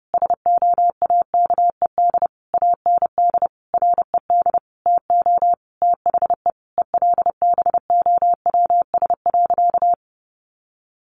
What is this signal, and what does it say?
Morse code, heard as a keyed tone, about 30 wpm, 700 Hz. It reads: SOAKED AND RED TO THE ELBOWS.
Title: A Sound Of Thunder 30wpm